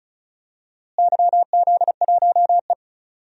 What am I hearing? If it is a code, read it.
Morse code, heard as a keyed tone, about 36 wpm, 700 Hz. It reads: YZ1E
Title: DXCC-100 Call Signs - 4 Characters - Repeated 3x - 35wpm